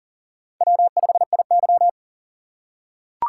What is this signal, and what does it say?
Morse code, heard as a keyed tone, about 40 wpm, 700 Hz. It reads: W5IY
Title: USA Call Signs - 4x Character Spacing - 40wpm